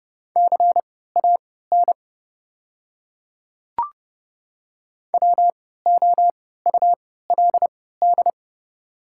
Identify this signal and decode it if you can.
Morse code, heard as a keyed tone, about 30 wpm, spaced as for 17 wpm, 690 Hz. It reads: CAN WOULD
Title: 3x Character Spacing - Top 100 Words - 30wpm